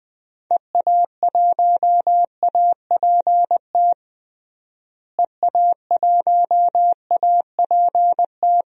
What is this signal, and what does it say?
Morse code, heard as a keyed tone, about 20 wpm, 700 Hz. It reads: EA1APT EA1APT
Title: DXCC-100 Call Signs - 6 Characters - Repeated 3x - 20wpm